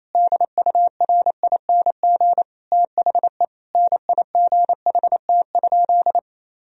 Morse code, heard as a keyed tone, about 28 wpm, 700 Hz. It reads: DURING THE NIGHT?
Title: Sentences from Top 1000 Words - Review Part 2 - 28wpm